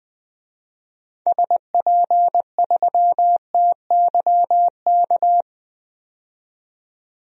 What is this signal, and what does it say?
Morse code, heard as a keyed tone, about 21 wpm, 700 Hz. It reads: SP3TYK
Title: DXCC-100 Call Signs - 6 Characters - 20wpm